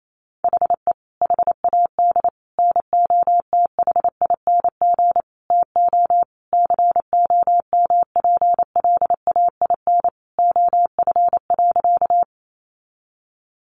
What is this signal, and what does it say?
Morse code, heard as a keyed tone, about 28 wpm, 700 Hz. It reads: HE HAD NOTHING TO COMPLAIN OF.
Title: Sentences from Top 1300 Words 28wpm